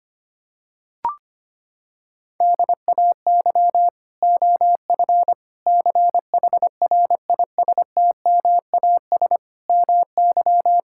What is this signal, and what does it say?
Morse code, heard as a keyed tone, about 25 wpm, 700 Hz. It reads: DAY OF CHRISTMAS MY
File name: Sets of 9 Words - 25wpm